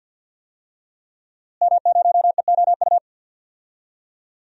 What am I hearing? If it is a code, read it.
Morse code, heard as a keyed tone, about 47 wpm, 700 Hz. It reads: M0EOU
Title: DXCC-100 Call Signs - 5 Characters - Repeated 3x - 50wpm